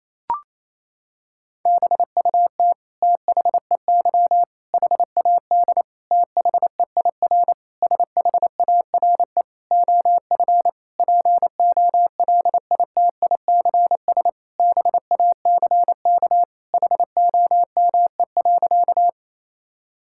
Morse code, heard as a keyed tone, about 28 wpm, 700 Hz. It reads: BUT THEY HAD THEIR SHARE OF POLITICS BACK HOME.
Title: Sentences from Top 1000 Words - Review Part 6 - 28wpm